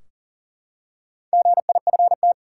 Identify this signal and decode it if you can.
Morse code, heard as a keyed tone, about 40 wpm, 700 Hz. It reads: GIFT
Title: Four Letter Words - 40wpm